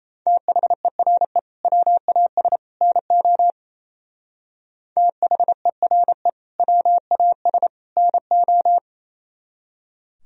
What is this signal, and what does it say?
Morse code, heard as a keyed tone, about 35 wpm, 700 Hz. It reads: THERE WAS NO THERE WAS NO
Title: Speed-Racing - Sets of 3 Words 28wpm